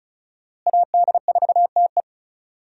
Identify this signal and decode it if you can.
Morse code, heard as a keyed tone, about 34 wpm, 700 Hz. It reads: AD4TE
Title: USA Call Signs - 5 Characters - Repeated 2x - 35wpm